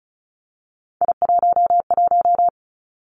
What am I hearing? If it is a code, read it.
Morse code, heard as a keyed tone, about 36 wpm, 700 Hz. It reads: I11
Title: Three Challenging Letter-Number Combinations 35wpm